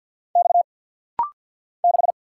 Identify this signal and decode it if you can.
Morse code, heard as a keyed tone, about 50 wpm, 700 Hz. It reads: X 6
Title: Single Letter-Number - Rapid-Fire 50wpm